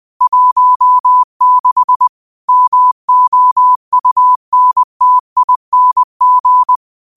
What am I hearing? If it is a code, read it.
Morse code, heard as a keyed tone, about 20 wpm, 1000 Hz. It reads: 16 MOUNTING
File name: qotdfiles / 20wpm